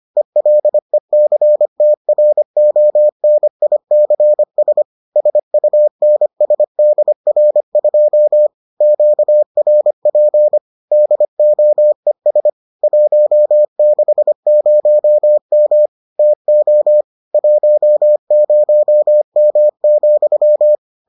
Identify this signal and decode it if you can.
Morse code, heard as a keyed tone, about 25 wpm, 600 Hz. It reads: ELECTRONICS SUNSDR2 QRP DOES 160M TO 10M,